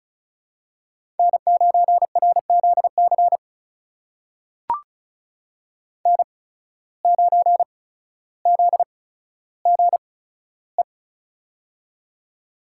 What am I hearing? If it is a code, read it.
Morse code, heard as a keyed tone, about 36 wpm, spaced as for 15 wpm, 700 Hz. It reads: N9RZC N9ZGE
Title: USA Call Signs - 8x Character Spacing - 35wpm